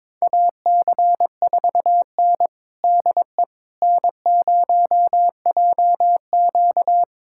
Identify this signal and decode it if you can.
Morse code, heard as a keyed tone, about 22 wpm, 700 Hz. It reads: AC4N DE N0JQ
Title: USA Callsign DE Callsign - 4 Characters - 22wpm